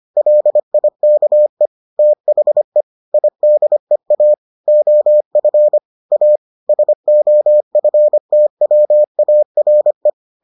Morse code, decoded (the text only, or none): LIKE THE IDEA OF A SOFTWARE